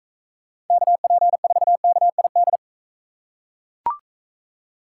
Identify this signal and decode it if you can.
Morse code, heard as a keyed tone, about 40 wpm, 700 Hz. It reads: KP4KID